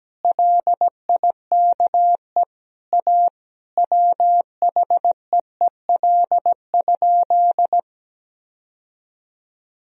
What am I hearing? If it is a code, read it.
Morse code, heard as a keyed tone, about 17 wpm, 700 Hz. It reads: LIKE A WHEEL?
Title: Sentences from Top 1000 Words - Review Part 3 - 17wpm